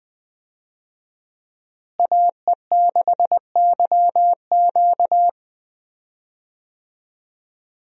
Morse code, heard as a keyed tone, about 19 wpm, 700 Hz. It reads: AE6YQ
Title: USA Call Signs - 5 Characters - Repeated 3x - 20wpm